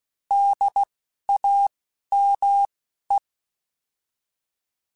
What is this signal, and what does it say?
Morse code, heard as a keyed tone, about 16 wpm, spaced as for 12 wpm, 800 Hz. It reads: DAME
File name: Alphabetical / Alphabetical mp3s / Ds @10